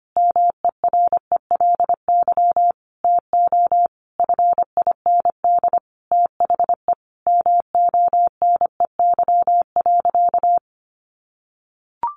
Morse code, heard as a keyed tone, about 25 wpm, 700 Hz. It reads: MERELY TO FIND THE MONEY.
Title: Sentences from Top 1400 Words 25wpm